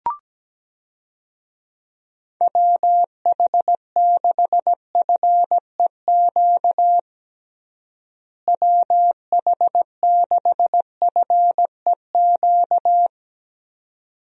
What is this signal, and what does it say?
Morse code, heard as a keyed tone, about 17 wpm, 700 Hz. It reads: WH6FEQ WH6FEQ
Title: USA Call Signs - 6 Characters - Repeated 2x - 17wpm